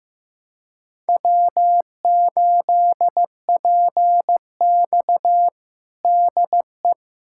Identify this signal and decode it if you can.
Morse code, heard as a keyed tone, about 16 wpm, 700 Hz. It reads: W8PX DE
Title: USA Callsign DE Callsign - 4 Characters - 15wpm